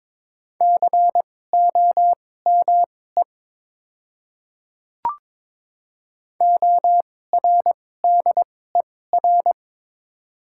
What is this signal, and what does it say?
Morse code, heard as a keyed tone, about 22 wpm, spaced as for 16 wpm, 700 Hz. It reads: COME ORDER
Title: 2x Character Spacing - Top 500 Words - 22wpm